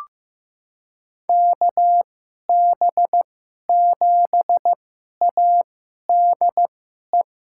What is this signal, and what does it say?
Morse code, heard as a keyed tone, about 15 wpm, spaced as for 11 wpm, 700 Hz. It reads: KB7ADE
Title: USA Call Signs - 2x Character Spacing - 15wpm